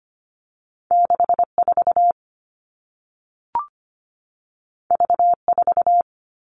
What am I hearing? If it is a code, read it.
Morse code, heard as a keyed tone, about 25 wpm, 700 Hz. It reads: 64 V4
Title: All Cut Numbers - 2-Digit 25wpm